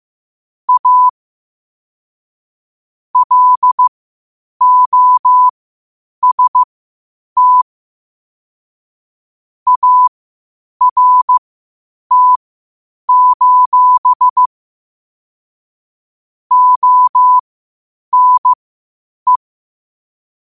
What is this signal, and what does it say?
Morse code, heard as a keyed tone, about 15 wpm, spaced as for 8 wpm, 1000 Hz. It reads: A LOST ART: ONE
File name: qotdfiles / 8wpm